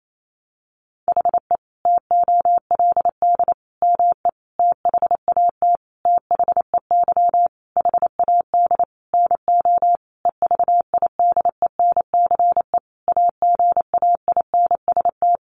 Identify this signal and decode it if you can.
Morse code, heard as a keyed tone, about 28 wpm, 700 Hz. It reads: HE TOLD ME THAT THEY HAD NO EVIDENCE AGAINST
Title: Sentences from Top 1300 Words 28wpm